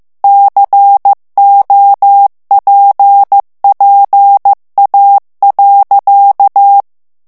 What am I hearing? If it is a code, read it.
Morse code, heard as a keyed tone, about 15 wpm, 800 Hz. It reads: COPPA.
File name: morse mp3 / ITALIEN texte clair 2 vitesse 14 mots minute